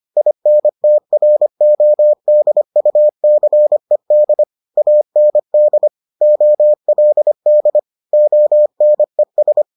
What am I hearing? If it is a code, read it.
Morse code, heard as a keyed tone, about 25 wpm, 600 Hz. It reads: INTRODUCED AND OLD ONES